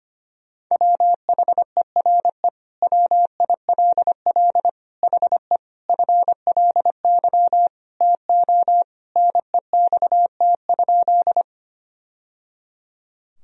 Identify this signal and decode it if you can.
Morse code, heard as a keyed tone, about 25 wpm, 700 Hz. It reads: WHERE WILL HE FLY TO NEXT?
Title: Sentences from Top 1000 Words - Review Part 2 - 25wpm